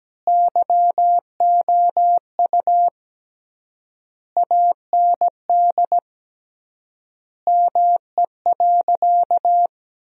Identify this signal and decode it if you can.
Morse code, heard as a keyed tone, about 17 wpm, 700 Hz. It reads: YOU AND ME.
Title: 3x Word-Spacing - Sentences from Top 500 Words - 17wpm